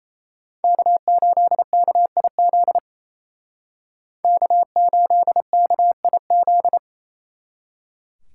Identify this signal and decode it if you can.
Morse code, heard as a keyed tone, about 36 wpm, 700 Hz. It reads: K8KIZ K8KIZ